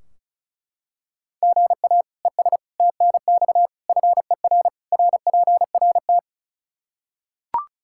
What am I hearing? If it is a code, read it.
Morse code, heard as a keyed tone, about 34 wpm, 700 Hz. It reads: GA ES TNX FER RPRT
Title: K1USN SST QSO Practice 35wpm